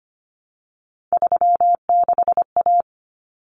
Morse code, heard as a keyed tone, about 25 wpm, 700 Hz. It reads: 36A
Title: Common Cut Numbers - 3-Digit 25wpm